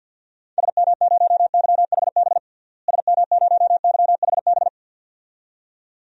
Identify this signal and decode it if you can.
Morse code, heard as a keyed tone, about 47 wpm, 700 Hz. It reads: SM0YHB SM0YHB
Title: DXCC-100 Call Signs - 6 Characters - Repeated 2x - 50wpm